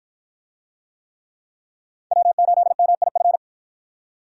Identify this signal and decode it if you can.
Morse code, heard as a keyed tone, about 48 wpm, 700 Hz. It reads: W8MIF